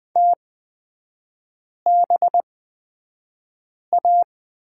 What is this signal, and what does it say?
Morse code, heard as a keyed tone, about 20 wpm, spaced as for 5 wpm, 700 Hz. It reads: TBA
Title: 255 Lesson 41e - Period - Three Characters 20wpm